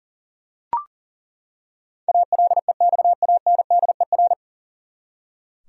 Morse code, heard as a keyed tone, about 40 wpm, 700 Hz. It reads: ALEXANDER
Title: USA Common Names - 40wpm